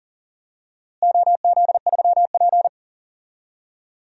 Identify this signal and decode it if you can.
Morse code, heard as a keyed tone, about 40 wpm, 700 Hz. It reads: OZ3P
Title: DXCC-100 Call Signs - 4 Characters - Repeated 3x - 40wpm